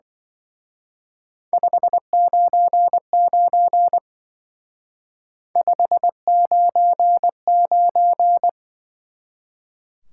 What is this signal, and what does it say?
Morse code, heard as a keyed tone, about 24 wpm, 700 Hz. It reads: 599 599
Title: Speed-Racing Top 180 QSO Elements 20wpm